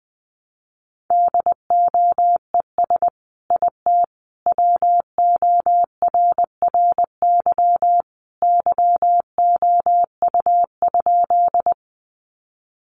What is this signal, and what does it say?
Morse code, heard as a keyed tone, about 20 wpm, 700 Hz. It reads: DOES IT WORRY YOU?
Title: Sentences from Top 1100 Words 20wpm